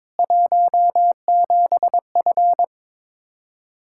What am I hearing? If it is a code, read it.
Morse code, heard as a keyed tone, about 21 wpm, 700 Hz. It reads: 17F
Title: Practice QSO - Exchange 2 - Info - 22wpm